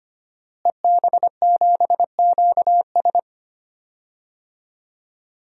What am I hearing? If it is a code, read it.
Morse code, heard as a keyed tone, about 25 wpm, 700 Hz. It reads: EB7QS
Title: DXCC-100 Call Signs - 5 Characters - Repeated 3x - 25wpm